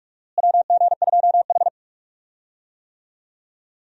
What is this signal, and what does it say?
Morse code, heard as a keyed tone, about 46 wpm, 700 Hz. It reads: WG2H
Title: USA Call Signs - 4 Characters - Repeated 2x - 45wpm